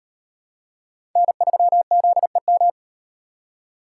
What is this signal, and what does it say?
Morse code, heard as a keyed tone, about 35 wpm, 700 Hz. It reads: N3ZEM